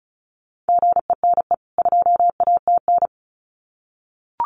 Morse code, heard as a keyed tone, about 34 wpm, 700 Hz. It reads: GENE 2ATN
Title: CWT Exchange 35wpm